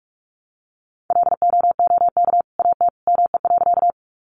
Morse code, heard as a keyed tone, about 44 wpm, 700 Hz. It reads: LOOK AT ME.
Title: Sentences from Top 200 Words 45wpm